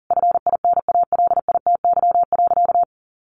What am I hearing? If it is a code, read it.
Morse code, heard as a keyed tone, about 40 wpm, 700 Hz. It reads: FINALITY.
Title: A Sound Of Thunder 40wpm